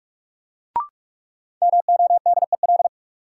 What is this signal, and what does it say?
Morse code, heard as a keyed tone, about 41 wpm, 700 Hz. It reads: MODEL
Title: Sentences from Top 1000 Words - New Word Drill - 45wpm